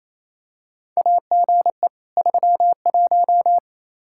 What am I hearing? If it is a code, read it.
Morse code, heard as a keyed tone, about 28 wpm, 700 Hz. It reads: AGE 31
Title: K1USN SST QSO Practice 28wpm